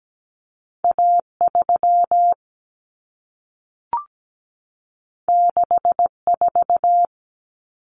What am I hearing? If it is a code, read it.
Morse code, heard as a keyed tone, about 17 wpm, 700 Hz. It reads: A3 64
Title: Common Cut Numbers - 2-Digit 17wpm